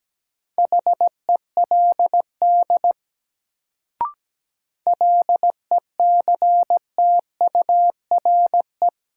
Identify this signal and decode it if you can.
Morse code, heard as a keyed tone, about 17 wpm, 700 Hz. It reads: HELD LECTURE
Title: Sentences from Top 1600 Words - New Word Drill - 17wpm